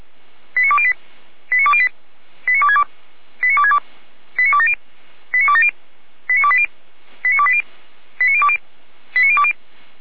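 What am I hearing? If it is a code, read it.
Five-tone selective call (ZVEI), heard as quick runs of five tones, tones of about 70 ms. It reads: 89208 89208 89262 89262 89270 89270 89280 89280 89020 89020
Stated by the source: ZVEI five-tone call received by SDR.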